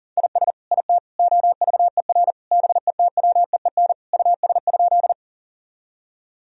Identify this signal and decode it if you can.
Morse code, heard as a keyed tone, about 37 wpm, 700 Hz. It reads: IS IT OVER BETWEEN US?
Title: Sentences from Top 1000 Words - Review Part 1 - 40wpm